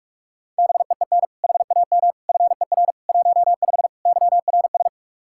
Morse code, heard as a keyed tone, about 41 wpm, 700 Hz. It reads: BEEN HAM FER 15 YRS
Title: Practice QSO - Exchange 3 - More Info - 45wpm